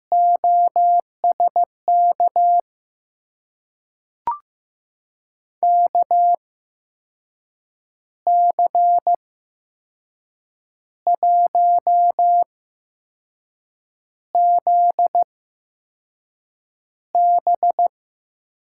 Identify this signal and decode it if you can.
Morse code, heard as a keyed tone, about 15 wpm, spaced as for 5 wpm, 700 Hz. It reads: OSK KC1ZB
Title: USA Call Signs - 5 Characters - 8x Character Spacing - 15wpm